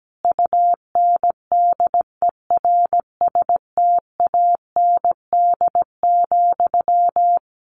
Morse code, heard as a keyed tone, about 17 wpm, 700 Hz. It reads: UNDERSTAND,
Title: Sentences from Top 600 Words 17wpm